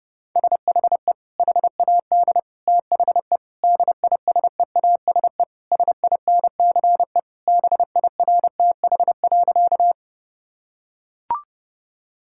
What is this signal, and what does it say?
Morse code, heard as a keyed tone, about 30 wpm, 700 Hz. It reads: SHE HAD THE DISEASE SINCE BIRTH.
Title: Sentences from Top 1000 Words - Review Part 6 - 30wpm